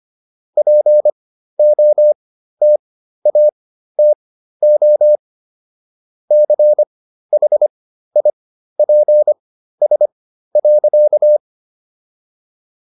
Morse code, heard as a keyed tone, about 25 wpm, spaced as for 13 wpm, 600 Hz. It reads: POTATO CHIPS.